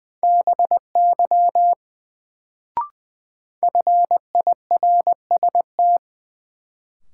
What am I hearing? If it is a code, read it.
Morse code, heard as a keyed tone, about 20 wpm, 700 Hz. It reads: BY FIRST
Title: Sets of 2 Words - New Word Drill - 20wpm